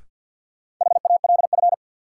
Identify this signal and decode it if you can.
Morse code, heard as a keyed tone, about 52 wpm, 700 Hz. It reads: HALF
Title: Sets of 3 Words - New Word Drill - 50wpm